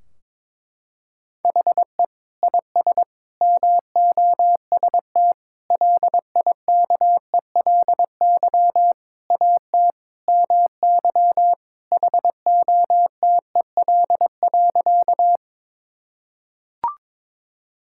Morse code, heard as a keyed tone, about 21 wpm, 700 Hz. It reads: HE IS MOST LIKELY AT MY HOTEL.
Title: Sentences from Top 1600 Words 22wpm